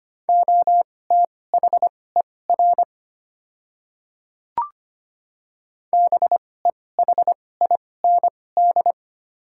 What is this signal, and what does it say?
Morse code, heard as a keyed tone, about 25 wpm, spaced as for 18 wpm, 700 Hz. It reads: OTHER BEHIND
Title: 2x Character Spacing - Top 500 Words - 25wpm